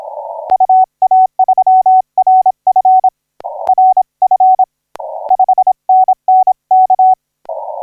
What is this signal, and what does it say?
Morse code, heard as a keyed tone, about 26 wpm, 750 Hz. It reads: UA3RF RF 5NN K
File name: D85C 0845UTC 15m